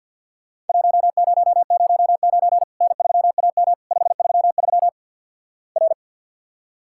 Morse code, heard as a keyed tone, about 50 wpm, 700 Hz. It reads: 1009 N3AM 533 R
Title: Worked All Europe DX Contest - QTC Exchange with Repeats - 50wpm